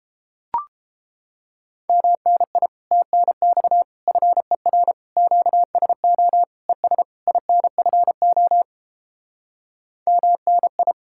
Morse code, heard as a keyed tone, about 31 wpm, 700 Hz. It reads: MNI TNX FER QSO ES INFO MNI